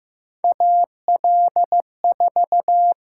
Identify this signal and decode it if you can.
Morse code, heard as a keyed tone, about 15 wpm, 700 Hz. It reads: AL4
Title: USA Call Signs - 4 Characters - 8x Character Spacing - 15wpm